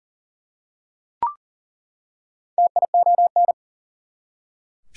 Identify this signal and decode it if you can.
Morse code, heard as a keyed tone, about 40 wpm, 700 Hz. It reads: TION
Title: Prefixes and Suffixes 40wpm